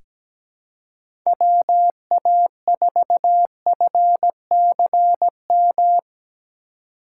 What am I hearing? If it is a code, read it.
Morse code, heard as a keyed tone, about 17 wpm, 700 Hz. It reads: WA4FCM
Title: USA Call Signs - 6 Characters - 3x Character Spacing - 17wpm